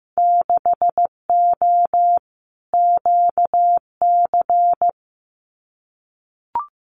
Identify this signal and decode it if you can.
Morse code, heard as a keyed tone, about 15 wpm, 700 Hz. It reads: 6O QC
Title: Winter Field Day Exchanges Encore 15wpm